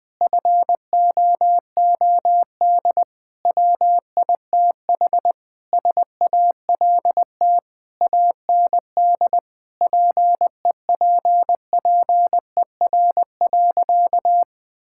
Morse code, heard as a keyed tone, about 20 wpm, 700 Hz. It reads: FOOD WITH SALT AND PEPPER.
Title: Sentences with 5-Word Phrases - Spoken Beginning - 20wpm